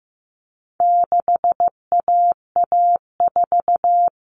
Morse code, heard as a keyed tone, about 15 wpm, 700 Hz. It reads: 6AA4
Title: All Cut Numbers - 4-Digit 15wpm